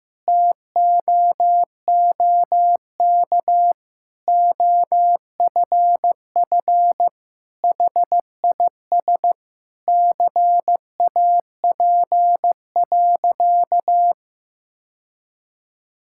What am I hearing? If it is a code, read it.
Morse code, heard as a keyed tone, about 15 wpm, 700 Hz. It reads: TOOK OFF HIS CAP.
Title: Sentences from Top 1600 Words 15wpm